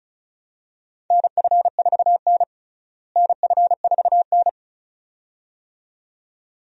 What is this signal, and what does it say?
Morse code, heard as a keyed tone, about 36 wpm, 700 Hz. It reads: NF4N NF4N
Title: USA Call Signs - 4 Characters - Repeated 2x - 35wpm